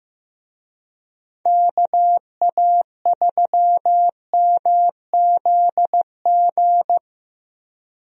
Morse code, heard as a keyed tone, about 15 wpm, 700 Hz. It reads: KA3MZG
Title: USA Call Signs - 6 Characters - Repeated 3x - 15wpm